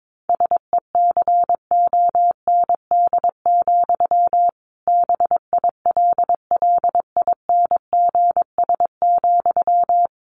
Morse code, heard as a keyed tone, about 22 wpm, 700 Hz. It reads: SECOND, BILLINGS,
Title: A Sound Of Thunder 22wpm